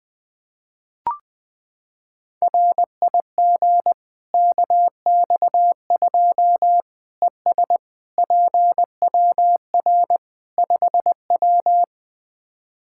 Morse code, heard as a keyed tone, about 20 wpm, 700 Hz. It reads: RIG KX2 ES PWR 5W
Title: Practice QSO - Exchange 2 - Info - 20wpm